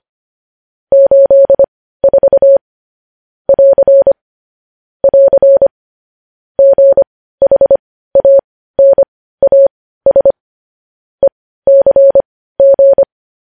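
Morse code, heard as a keyed tone, about 25 wpm, spaced as for 15 wpm, 550 Hz. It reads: 84 <AR> <AR> GHANAS ECG
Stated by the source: Morse code sending of news headlines at 15 wpm.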